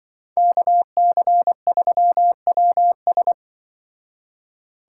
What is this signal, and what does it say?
Morse code, heard as a keyed tone, about 25 wpm, 700 Hz. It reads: KC3WS
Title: Speed Racing - USA 2x2 Call Signs 20wpm